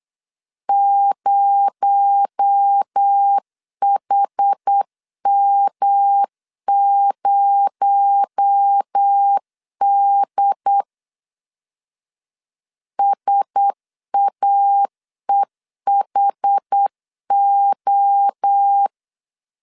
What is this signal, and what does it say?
Morse code, heard as a keyed tone, about 8 wpm, 800 Hz. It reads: 0HM0D SAEHO
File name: LEZIONI / 6 PKX0 / 6 PKX0 8wpm 3341